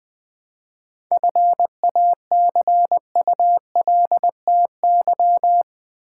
Morse code, heard as a keyed tone, about 20 wpm, 700 Hz. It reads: FACULTY
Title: 7 Letter Words - First 3 Letters Given - 20wpm